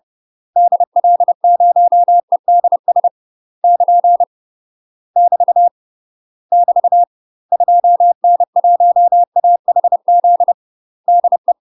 Morse code, heard as a keyed tone, about 30 wpm, 700 Hz. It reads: DL0EDS <KN> <BT> <BT> 2N1AHZ DE
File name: UK MASTER QSOs TRAINING/30WPM